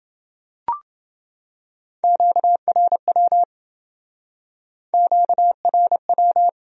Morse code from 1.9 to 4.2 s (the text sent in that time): QRW